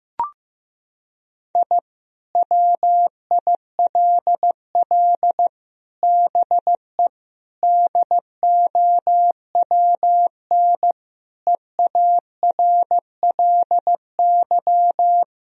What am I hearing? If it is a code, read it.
Morse code, heard as a keyed tone, about 15 wpm, 700 Hz. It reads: I WILL BE DOWN EARLY
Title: Sentences from Top 1000 Words - Review Part 2 - 15wpm